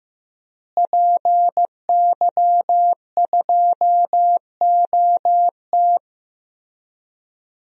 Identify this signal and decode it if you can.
Morse code, heard as a keyed tone, about 15 wpm, 700 Hz. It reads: PY2OT
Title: DXCC-100 Call Signs - 5 Characters - Repeated 3x - 15wpm